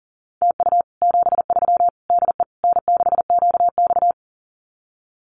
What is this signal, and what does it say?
Morse code, heard as a keyed tone, about 40 wpm, 700 Hz. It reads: TU 73 DE N6QX
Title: Common QSO Phrases Encore 40wpm